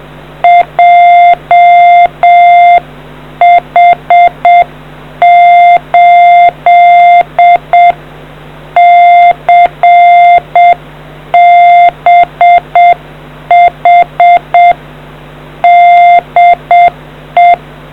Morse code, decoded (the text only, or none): JH8CBH DE